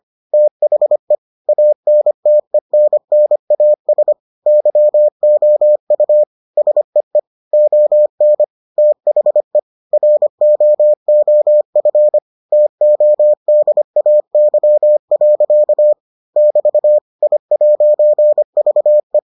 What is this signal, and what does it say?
Morse code, heard as a keyed tone, about 25 wpm, 600 Hz. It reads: THE ANTENNAS YOU SEE ON THE ROOF TODAY. <BT> I'VE